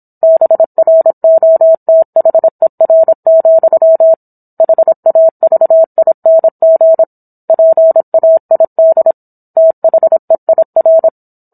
Morse code, heard as a keyed tone, about 26 wpm, 650 Hz. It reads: BROTHER, HAVING PAID THEIR